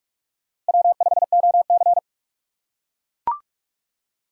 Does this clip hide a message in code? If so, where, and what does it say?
Yes: Morse code from 0.7 to 2.0 s, heard as a keyed tone, about 44 wpm, 700 Hz. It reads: W5OC